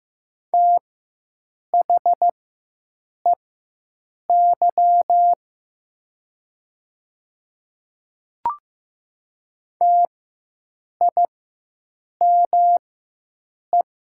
Morse code, heard as a keyed tone, about 14 wpm, spaced as for 7 wpm, 700 Hz. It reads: THEY TIME
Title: 4x Character Spacing - Top 500 Words - 15wpm